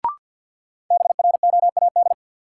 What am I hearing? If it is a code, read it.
Morse code, heard as a keyed tone, about 51 wpm, 700 Hz. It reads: BROAD
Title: Sentences from Top 1000 Words - New Word Drill - 50wpm